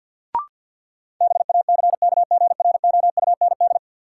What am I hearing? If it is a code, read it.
Morse code, heard as a keyed tone, about 52 wpm, 700 Hz. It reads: BACKGROUND
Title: Sentences from Top 1300 Words - New Word Drill - 50wpm